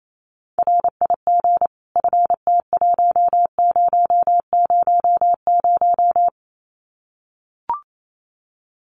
Morse code, heard as a keyed tone, about 28 wpm, 700 Hz. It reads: RIG FT1000
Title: Common QSO Phrases 28wpm